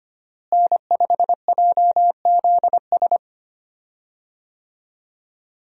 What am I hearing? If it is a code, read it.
Morse code, heard as a keyed tone, about 25 wpm, 700 Hz. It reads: N5JZS
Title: USA Call Signs - 5 Characters - Repeated 2x - 25wpm